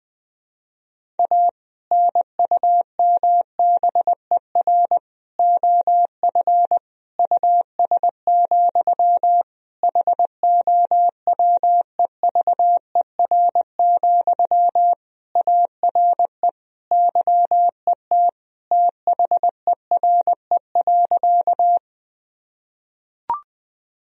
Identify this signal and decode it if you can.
Morse code, heard as a keyed tone, about 20 wpm, 700 Hz. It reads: A NUMBER OF US, HOWEVER, ARE YET THERE.
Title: Sentences from Top 1000 Words - Review Part 1 - 20wpm